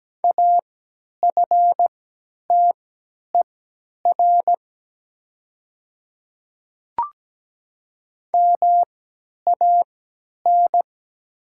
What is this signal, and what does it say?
Morse code, heard as a keyed tone, about 17 wpm, spaced as for 10 wpm, 700 Hz. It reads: AFTER MAN
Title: 3x Character Spacing - Top 300 Words - 17wpm